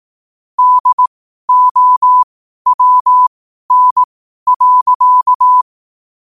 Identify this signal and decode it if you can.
Morse code, heard as a keyed tone, about 18 wpm, spaced as for 12 wpm, 1000 Hz. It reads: DOWN.